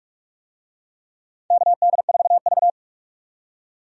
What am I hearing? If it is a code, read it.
Morse code, heard as a keyed tone, about 46 wpm, 700 Hz. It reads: KD4V
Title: USA Call Signs - 4 Characters - 45wpm